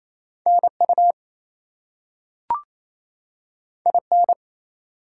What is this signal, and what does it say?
Morse code, heard as a keyed tone, about 28 wpm, 700 Hz. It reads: NU IN
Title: US States and Canadian Provinces 28wpm